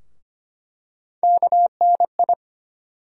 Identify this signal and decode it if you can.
Morse code, heard as a keyed tone, about 25 wpm, 700 Hz. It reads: KNI
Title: USA Call Sign Postfixes - 3 Letters 25wpm